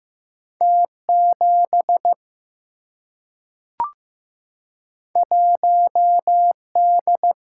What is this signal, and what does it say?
Morse code, heard as a keyed tone, about 15 wpm, 700 Hz. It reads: T 7 1 D
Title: Single-Letter Single-Number 15wpm